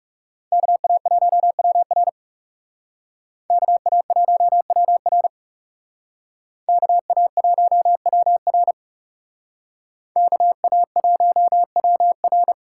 Morse code, heard as a keyed tone, about 39 wpm, 700 Hz. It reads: KA1WR KA1WR KA1WR KA1WR